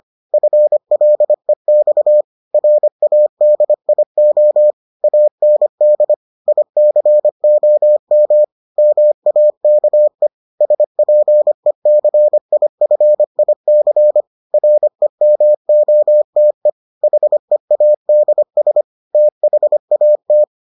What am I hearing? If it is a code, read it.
Morse code, heard as a keyed tone, about 25 wpm, 600 Hz. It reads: FLEX RADIO AND ICOM MAKE SPECIFIC REMOTE HEADS THAT